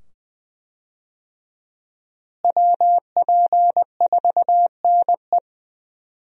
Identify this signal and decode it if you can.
Morse code, heard as a keyed tone, about 20 wpm, 700 Hz. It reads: WP4NE
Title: USA Call Signs - 5 Characters - Repeated 2x - 20wpm